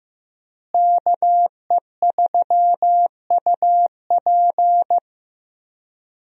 Morse code, heard as a keyed tone, about 15 wpm, 700 Hz. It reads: KE3UP
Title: USA Call Signs - 5 Characters - Repeated 3x - 15wpm